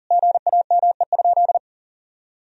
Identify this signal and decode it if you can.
Morse code, heard as a keyed tone, about 40 wpm, 700 Hz. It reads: GAME?
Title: Sentences from Top 1000 Words - Review Part 4 - 40wpm